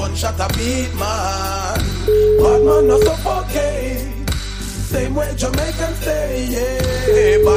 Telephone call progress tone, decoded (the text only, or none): ringback tone